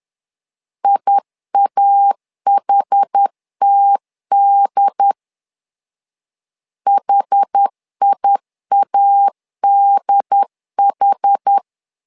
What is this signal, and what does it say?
Morse code, heard as a keyed tone, about 10 wpm, 800 Hz. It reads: IAHTD HIADH
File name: LEZIONI / 2 UHDN / 2 UHD 10wpm 1826